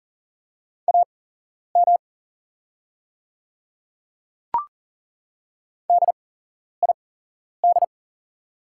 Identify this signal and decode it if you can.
Morse code, heard as a keyed tone, about 40 wpm, spaced as for 11 wpm, 700 Hz. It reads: AM DID
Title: 8x Character Spacing - Top 500 Words - 40wpm